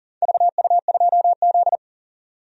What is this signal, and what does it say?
Morse code, heard as a keyed tone, about 40 wpm, 700 Hz. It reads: VU2Z
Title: DXCC-100 Call Signs - 4 Characters - 40wpm